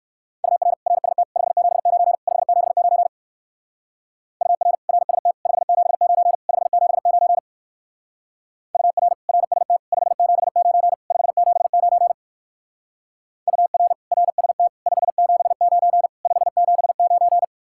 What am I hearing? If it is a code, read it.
Morse code, heard as a keyed tone, about 54 wpm, 700 Hz. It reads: UR RST 579 579 UR RST 579 579 UR RST 579 579 UR RST 579 579
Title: Speed-Racing Common QSO Phrases - 45wpm